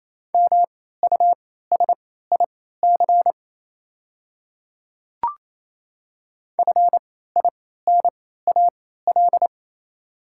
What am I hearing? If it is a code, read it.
Morse code, heard as a keyed tone, about 28 wpm, spaced as for 16 wpm, 700 Hz. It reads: MUSIC FINAL
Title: 3x Character Spacing - Top 500 Words - 28wpm